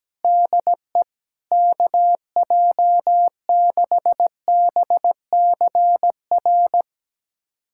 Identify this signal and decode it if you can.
Morse code, heard as a keyed tone, about 17 wpm, 700 Hz. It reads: DE KJ6BCR
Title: USA Callsign DE Callsign - 6 Characters - 17wpm